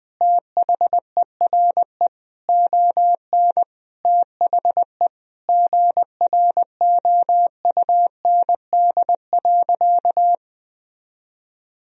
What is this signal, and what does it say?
Morse code, heard as a keyed tone, about 20 wpm, 700 Hz. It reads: THERE ON THE GROUND.
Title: Sentences from Top 1000 Words - Review Part 2 - 20wpm